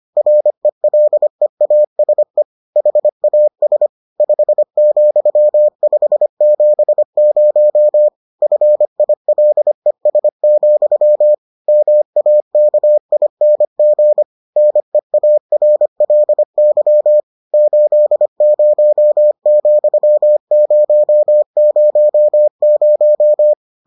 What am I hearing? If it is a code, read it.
Morse code, heard as a keyed tone, about 25 wpm, 600 Hz. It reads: RELEASE HAS 5,570 FILES, MAKING NEARLY 80,000